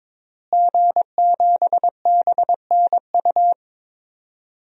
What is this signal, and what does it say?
Morse code, heard as a keyed tone, about 22 wpm, 700 Hz. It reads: G7BNU
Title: DXCC-100 Call Signs - 5 Characters - Repeated 3x - 22wpm